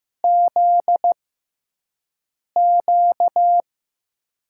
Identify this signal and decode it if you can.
Morse code, heard as a keyed tone, about 15 wpm, spaced as for 5 wpm, 700 Hz. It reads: ZQ
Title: USA Call Signs - 6x Character Spacing - 15wpm